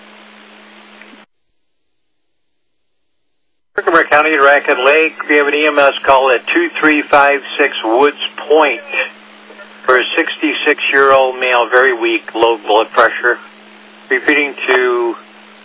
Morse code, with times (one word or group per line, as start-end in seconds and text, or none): none